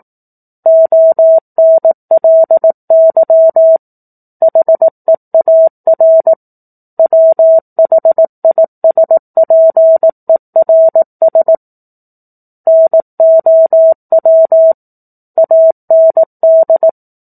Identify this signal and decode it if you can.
Morse code, heard as a keyed tone, about 18 wpm, 650 Hz. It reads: ONLY HEAR WHISPERS NOW AND
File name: 04-Chapter-04 18 WPM